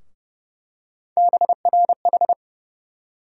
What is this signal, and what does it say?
Morse code, heard as a keyed tone, about 31 wpm, 700 Hz. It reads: BRH
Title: USA Call Sign Postfixes - 3 Letters 30wpm